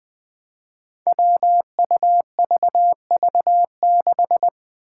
Morse code, heard as a keyed tone, about 20 wpm, 700 Hz. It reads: WUVV6
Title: Five Challenging Letter-Number Combinations 20wpm